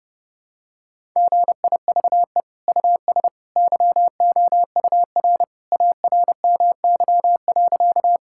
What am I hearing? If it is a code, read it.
Morse code, heard as a keyed tone, about 30 wpm, 700 Hz. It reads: GIVE US YOUR ARMY.
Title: Sentences from Top 1000 Words - Review Part 5 - 30wpm